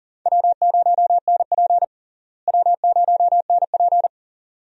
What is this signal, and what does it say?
Morse code, heard as a keyed tone, about 40 wpm, 700 Hz. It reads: W0NP W0NP
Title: USA Call Signs - 4 Characters - Repeated 2x - 40wpm